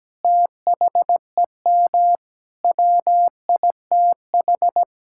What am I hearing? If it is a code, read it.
Morse code, heard as a keyed tone, about 17 wpm, 700 Hz. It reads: THEM WITH
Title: Sentences from Top 1600 Words 17wpm